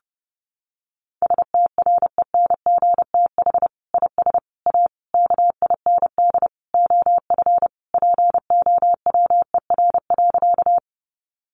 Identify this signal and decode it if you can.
Morse code, heard as a keyed tone, about 30 wpm, 700 Hz. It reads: STRENGTH IS A KIND OF POWER.
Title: Sentences from Top 1300 Words 30wpm